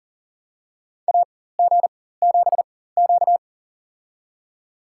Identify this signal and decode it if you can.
Morse code, heard as a keyed tone, about 40 wpm, spaced as for 19 wpm, 700 Hz. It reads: AG7Q
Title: USA Call Signs - 4 Characters - 4x Character Spacing - 40wpm